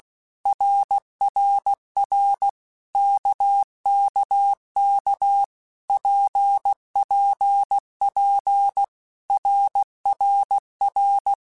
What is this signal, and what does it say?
Morse code, heard as a keyed tone, about 16 wpm, 800 Hz. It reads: RRR KKK PPP RRR
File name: Basic Letter Groups By7Groups / RKPX / RKPX@15